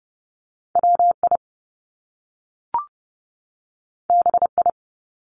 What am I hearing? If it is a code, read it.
Morse code, heard as a keyed tone, about 30 wpm, 700 Hz. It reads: WI BI
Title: Top Two Letter Combinations in Words 30wpm